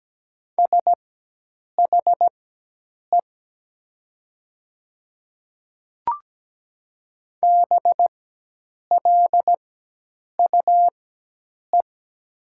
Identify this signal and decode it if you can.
Morse code, heard as a keyed tone, about 17 wpm, spaced as for 8 wpm, 700 Hz. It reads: SHE BLUE
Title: 4x Character Spacing - Top 500 Words - 17wpm